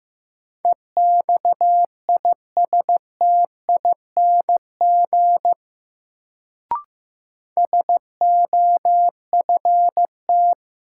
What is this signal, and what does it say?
Morse code, heard as a keyed tone, about 15 wpm, 700 Hz. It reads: EXISTING SOFT
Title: Sentences from Top 1500 Words - New Word Drill Alt - 15wpm